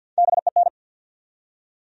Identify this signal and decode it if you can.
Morse code, heard as a keyed tone, about 51 wpm, 700 Hz. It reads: BEN
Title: USA Common Names - Max 5 Letters - 50wpm